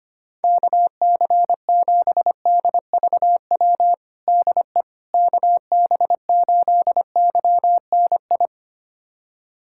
Morse code, heard as a keyed tone, about 25 wpm, 700 Hz. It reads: KC7DVW DE KB8YNI
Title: USA Callsign DE Callsign 25wpm